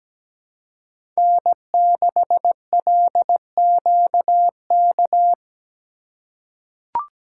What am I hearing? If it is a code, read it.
Morse code, heard as a keyed tone, about 17 wpm, 700 Hz. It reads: N6LQK
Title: USA Call Signs - 5 Characters - 4x Character Spacing - 17wpm